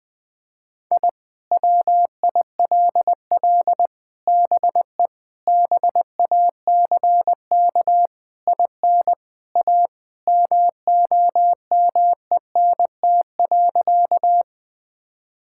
Morse code, heard as a keyed tone, about 20 wpm, 700 Hz. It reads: I WILL BE BACK IN A MOMENT.
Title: Sentences with 5-Word Phrases - 20wpm